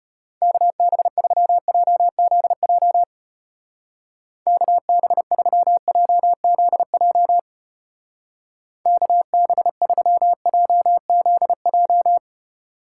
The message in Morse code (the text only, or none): KB3JZJ KB3JZJ KB3JZJ